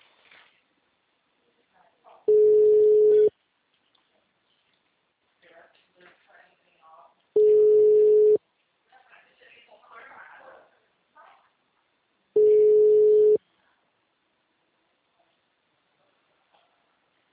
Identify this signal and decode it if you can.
Ringback tone (Europe, 425 Hz), with speech.
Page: TLML PRUDENTE 1